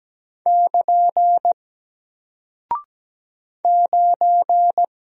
Morse code, heard as a keyed tone, about 17 wpm, 700 Hz. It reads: <KN> 9
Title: Single Character (Letter:Number:Punctuation:Prosign) Alt 17wpm